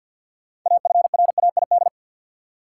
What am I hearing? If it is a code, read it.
Morse code, heard as a keyed tone, about 51 wpm, 700 Hz. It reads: AFRAID
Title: Sentences from Top 1500 Words - New Word Drill Alt - 50wpm